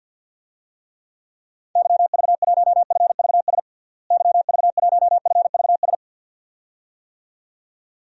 Morse code, heard as a keyed tone, about 51 wpm, 700 Hz. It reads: YV1FVS YV1FVS
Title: DXCC-100 Call Signs - 6 Characters - Repeated 2x - 50wpm